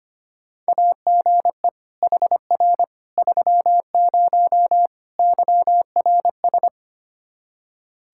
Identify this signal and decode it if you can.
Morse code, heard as a keyed tone, about 25 wpm, 700 Hz. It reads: AGE HR 30 YRS
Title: Practice QSO - Exchange 3 - More Info - 25wpm